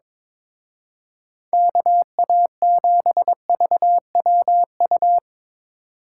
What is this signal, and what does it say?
Morse code, heard as a keyed tone, about 25 wpm, 700 Hz. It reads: KA7VWU
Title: Speed Racing - USA 2x3 Call Signs 22wpm